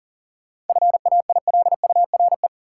Morse code, heard as a keyed tone, about 40 wpm, 700 Hz. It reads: FAILURE
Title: Sentences from Top 900 Words - New Word Drill - 40wpm